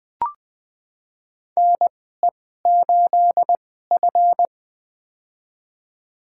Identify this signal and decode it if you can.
Morse code, heard as a keyed tone, about 20 wpm, spaced as for 15 wpm, 700 Hz. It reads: NE8F
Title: USA Call Signs - 4 Characters - 2x Character Spacing - 20wpm